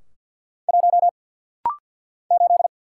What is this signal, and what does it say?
Morse code, heard as a keyed tone, about 48 wpm, 700 Hz. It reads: 1 8
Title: Single Letter-Number - Rapid-Fire 50wpm